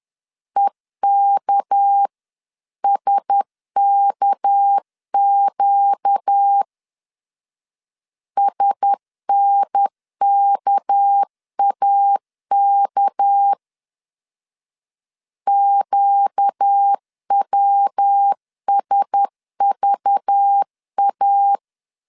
Morse code, heard as a keyed tone, about 10 wpm, 800 Hz. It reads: EKSKQ SNKAK QWSVA
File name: LEZIONI / 6 PKX0 / 6 PK 10wpm 2436